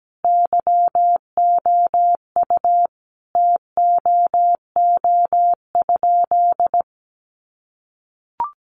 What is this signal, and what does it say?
Morse code, heard as a keyed tone, about 17 wpm, 700 Hz. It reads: YOU TOO?
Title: Sentences from Top 1500 Words 17wpm